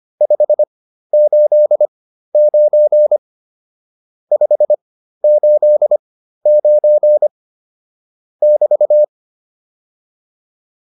Morse code, heard as a keyed tone, about 25 wpm, spaced as for 13 wpm, 600 Hz. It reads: 589 589 <BT>